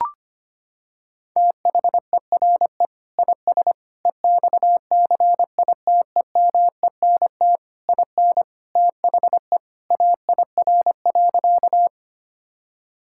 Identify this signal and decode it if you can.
Morse code, heard as a keyed tone, about 25 wpm, 700 Hz. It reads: THERE IS EXCITEMENT IN THE AIR.
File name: practice / 25wpm / Sentences with 3-Word Phrases - 25wpm